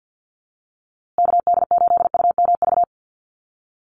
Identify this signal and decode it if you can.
Morse code, heard as a keyed tone, about 50 wpm, 700 Hz. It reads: KD8UMV
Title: USA Call Signs Repeated 2x 50wpm